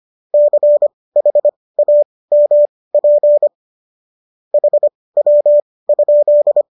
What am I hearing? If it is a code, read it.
Morse code, heard as a keyed tone, about 25 wpm, spaced as for 18 wpm, 590 Hz. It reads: CHAMP HW?